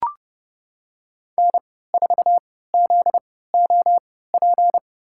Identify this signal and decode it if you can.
Morse code, heard as a keyed tone, about 30 wpm, spaced as for 17 wpm, 700 Hz. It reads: N4ZOP
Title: USA Call Signs - 5 Characters - 3x Character Spacing - 30wpm